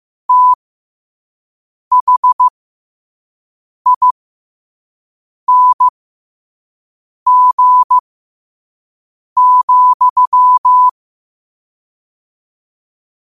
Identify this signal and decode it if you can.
Morse code, heard as a keyed tone, about 15 wpm, spaced as for 5 wpm, 1000 Hz. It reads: THING,
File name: qotdfiles / 5wpm